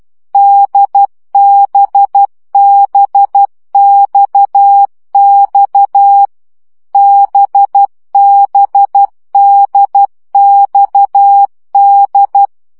Morse code, heard as a keyed tone, about 12 wpm, 800 Hz. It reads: DBBXX BBDXD
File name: CW gyakorlas d b x 10 baud